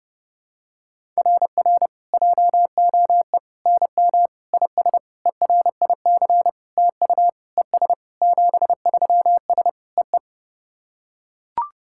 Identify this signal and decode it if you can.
Morse code, heard as a keyed tone, about 30 wpm, 700 Hz. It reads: RR JOE NM IS ERIC TU ES 73S EE
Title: SOTA QSO Practice 30wpm